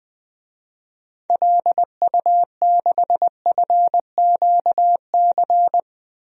Morse code, heard as a keyed tone, about 21 wpm, 700 Hz. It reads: LU6FQC
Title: DXCC-100 Call Signs - 6 Characters - 20wpm